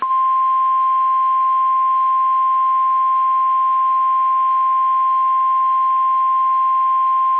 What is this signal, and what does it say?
Emergency Alert System (SAME) tones (the 1050 Hz attention tone). It is heard from 0.3 to 6.7 s.
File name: eas / KSDZKDJL / KSDZKDJL